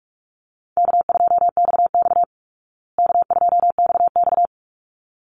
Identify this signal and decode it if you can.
Morse code, heard as a keyed tone, about 44 wpm, 700 Hz. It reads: K2XX K2XX
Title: Easy Call Signs Repeated 2x - 45wpm